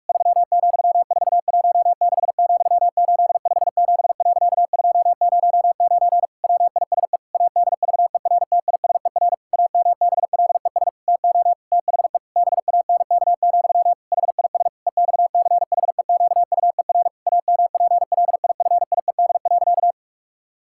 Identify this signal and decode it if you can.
Morse code, heard as a keyed tone, about 46 wpm, 700 Hz. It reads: 3,416,857.209 WISE ADVERTISER AMBLES TO THE BANK, HIS EXCHEQUER AMPLIFIED.
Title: Pangram Challenge v2 - 45wpm